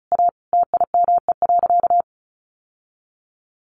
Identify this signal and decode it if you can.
Morse code, heard as a keyed tone, about 34 wpm, 700 Hz. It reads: A TIME.
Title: Sentences from Top 1100 Words 35wpm